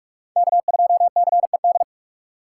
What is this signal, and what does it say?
Morse code, heard as a keyed tone, about 44 wpm, 700 Hz. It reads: K2CED